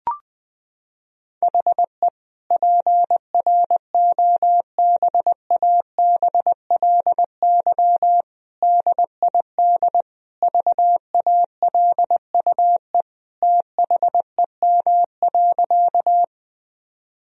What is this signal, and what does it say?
Morse code, heard as a keyed tone, about 20 wpm, 700 Hz. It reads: HE PROBABLY DID VALUE THEM.
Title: Sentences from Top 1000 Words - Review Part 4 - 20wpm